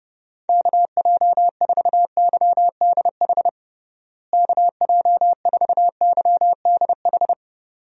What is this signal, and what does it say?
Morse code, heard as a keyed tone, about 32 wpm, 700 Hz. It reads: KJ4YDH KJ4YDH
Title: USA Call Signs - 6 Characters - Repeated 2x - 30wpm